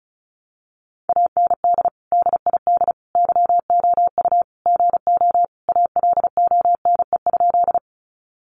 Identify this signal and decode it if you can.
Morse code, heard as a keyed tone, about 34 wpm, 700 Hz. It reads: AND DID YOU GO ALONE?
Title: Sentences from Top 1000 Words 35wpm